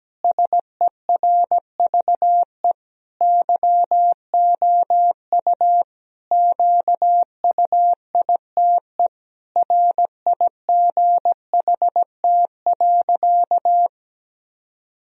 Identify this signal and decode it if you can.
Morse code, heard as a keyed tone, about 17 wpm, 700 Hz. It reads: SERVE YOU QUITE RIGHT.
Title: Sentences from Top 1000 Words - Review Part 2 - 17wpm